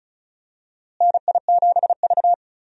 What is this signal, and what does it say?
Morse code, heard as a keyed tone, about 36 wpm, 700 Hz. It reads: NI7V
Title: USA Call Signs - 4 Characters - Repeated 3x - 35wpm